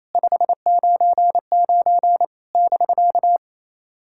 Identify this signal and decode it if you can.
Morse code, heard as a keyed tone, about 27 wpm, 690 Hz. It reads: 599 <BK>
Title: SOTA QSO Practice 28wpm